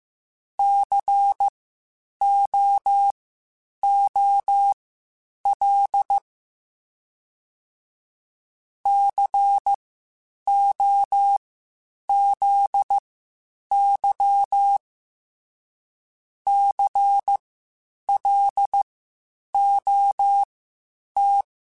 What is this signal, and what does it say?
Morse code, heard as a keyed tone, about 15 wpm, spaced as for 8 wpm, 800 Hz. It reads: COOL COZY CLOT
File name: Spaced Alphas / Spaced Alphas mp3s / Cs sp @13